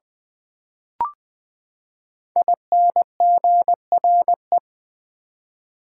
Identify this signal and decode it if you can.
Morse code, heard as a keyed tone, about 20 wpm, 700 Hz. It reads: INGRE
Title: Build A Word - 5 Letters At A Time - 10-Letter Words - 20wpm